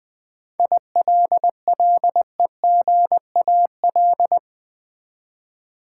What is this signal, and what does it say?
Morse code, heard as a keyed tone, about 20 wpm, 700 Hz. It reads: ILLEGAL
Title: Seven Letter Words - 20wpm